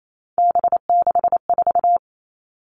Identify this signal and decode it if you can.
Morse code, heard as a keyed tone, about 28 wpm, 700 Hz. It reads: B64
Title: All Cut Numbers - 3-Digit 28wpm